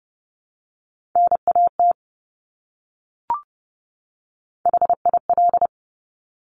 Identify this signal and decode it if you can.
Morse code, heard as a keyed tone, about 30 wpm, 700 Hz. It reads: NAT HIL
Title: Top Three Letter Combinations in Words 30wpm